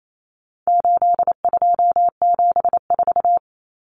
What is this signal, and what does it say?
Morse code, heard as a keyed tone, about 28 wpm, 700 Hz. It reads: 8274
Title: Numbers - 4-Digit 28wpm